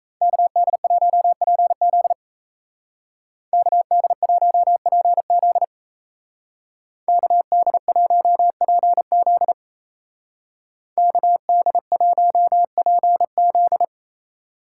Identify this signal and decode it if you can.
Morse code, heard as a keyed tone, about 36 wpm, 700 Hz. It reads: KD1PZ KD1PZ KD1PZ KD1PZ